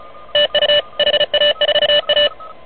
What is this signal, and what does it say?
Morse code, heard as a keyed tone, about 28 wpm, 590 Hz, with noise. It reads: TU HA4A
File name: truSDX